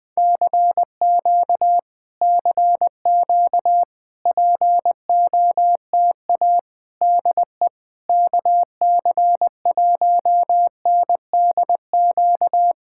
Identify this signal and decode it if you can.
Morse code, heard as a keyed tone, about 21 wpm, 700 Hz. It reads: CQ CQ POTA DE KC1NDQ
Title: POTA QSO Practice - 20wpm